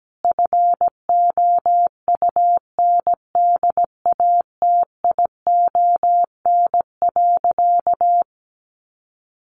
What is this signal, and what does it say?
Morse code, heard as a keyed tone, about 17 wpm, 700 Hz. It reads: FOUNDATION.
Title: Sentences from Top 1200 Words 17wpm